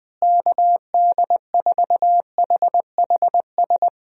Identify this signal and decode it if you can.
Morse code, heard as a keyed tone, about 21 wpm, 700 Hz. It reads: KD4HHS
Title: USA Call Signs - 6 Characters - Repeated 2x - 20wpm